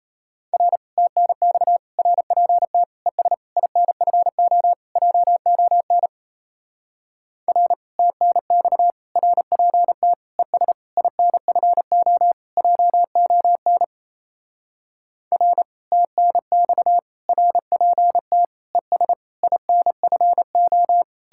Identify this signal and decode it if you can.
Morse code, heard as a keyed tone, about 37 wpm, 700 Hz. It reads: R TNX RPT ES INFO JON R TNX RPT ES INFO JON R TNX RPT ES INFO